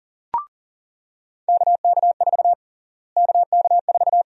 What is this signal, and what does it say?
Morse code, heard as a keyed tone, about 40 wpm, 700 Hz. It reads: KK4 KK4
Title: USA Call Sign Prefixes - Repeated 2x - 40wpm